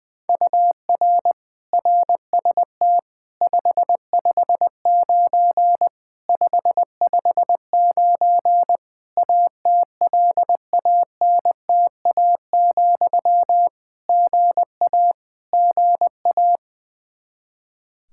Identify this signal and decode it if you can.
Morse code, heard as a keyed tone, about 20 wpm, 700 Hz. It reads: UR RST 559 559 ATLANTA, GA GA
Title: K1USN SST QSO Practice 20wpm